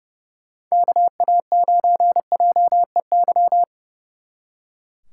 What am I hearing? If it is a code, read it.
Morse code, heard as a keyed tone, about 29 wpm, 700 Hz. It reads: KA9JEY
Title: Speed Racing - USA Call Signs 30wpm